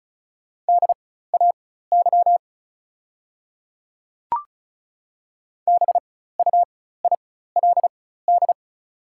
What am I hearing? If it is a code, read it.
Morse code, heard as a keyed tone, about 36 wpm, spaced as for 16 wpm, 700 Hz. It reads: DAY BUILD
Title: 4x Character Spacing - Top 200 Words - 35wpm